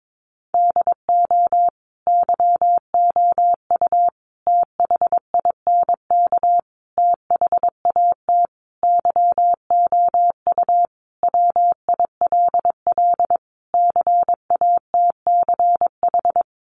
Morse code, heard as a keyed tone, about 22 wpm, 700 Hz. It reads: DO YOU THINK THAT YOU WILL CATCH
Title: Sentences from Top 1500 Words 22wpm